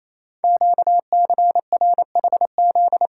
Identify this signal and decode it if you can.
Morse code, heard as a keyed tone, about 29 wpm, 700 Hz. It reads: QCRHZ
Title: Code Groups - 5 Letters - 28wpm